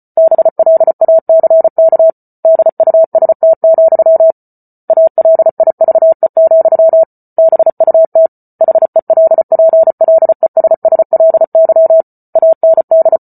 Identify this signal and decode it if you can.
Morse code, heard as a keyed tone, about 34 wpm, 650 Hz. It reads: BLACK DUST, ALIVE, BUT HELPLESSLY AND